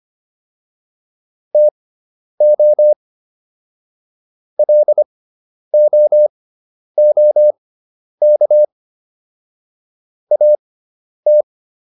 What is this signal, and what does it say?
Morse code, heard as a keyed tone, about 25 wpm, spaced as for 10 wpm, 600 Hz. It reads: TO LOOK AT